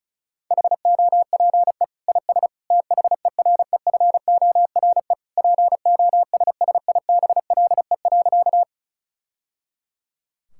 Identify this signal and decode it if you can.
Morse code, heard as a keyed tone, about 36 wpm, 700 Hz. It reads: HOPE IS THEREFORE POSSIBLE.
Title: Sentences from Top 1000 Words - Review Part 5 - 35wpm